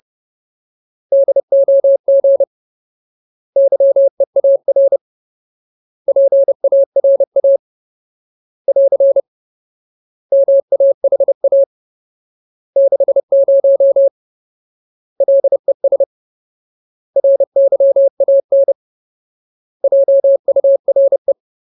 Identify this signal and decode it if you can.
Morse code, heard as a keyed tone, about 30 wpm, 550 Hz. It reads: DOG YEAR PARA <AR> MAHA 60 LES RYAN JURE
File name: Ultimate Word List / 30WPM Ultimate